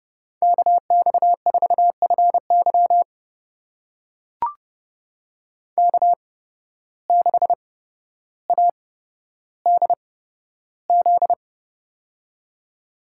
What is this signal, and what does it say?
Morse code, heard as a keyed tone, about 30 wpm, spaced as for 13 wpm, 700 Hz. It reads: KX4FY K6ADZ
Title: USA Call Signs - 8x Character Spacing - 30wpm